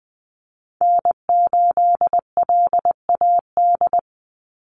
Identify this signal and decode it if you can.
Morse code, heard as a keyed tone, about 20 wpm, 700 Hz. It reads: N8LAD
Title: USA Call Signs 20wpm